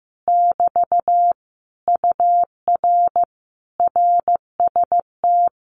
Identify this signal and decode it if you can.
Morse code, heard as a keyed tone, about 15 wpm, 700 Hz. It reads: <BT> UR RST
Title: Common QSO Phrases Encore 15wpm